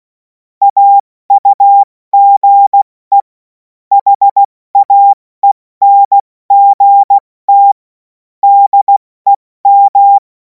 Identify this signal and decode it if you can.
Morse code, heard as a keyed tone, about 16 wpm, 800 Hz. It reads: AUGE HAENGT DEM
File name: Morsekurs / Witze / Witz 17. Februar 2026 70 F80